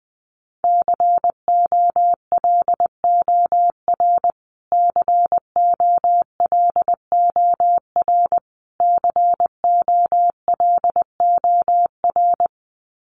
Morse code, heard as a keyed tone, about 20 wpm, 700 Hz. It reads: COLOR COLOR COLOR
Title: Top 300 Words Repeated 20wpm